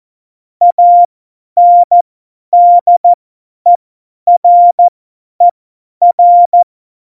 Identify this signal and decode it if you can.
Morse code, heard as a keyed tone, about 14 wpm, spaced as for 10 wpm, 700 Hz. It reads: ANDERER
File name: Morsekurs / Witze / Witz 24.Dezember 2019, 50 F70